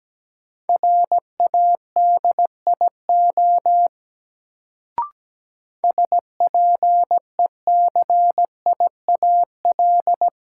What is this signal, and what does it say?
Morse code, heard as a keyed tone, about 17 wpm, 700 Hz. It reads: RADIO SPECIAL
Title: Sentences from Top 1000 Words - Review Part 3 - New Word Drill - 17wpm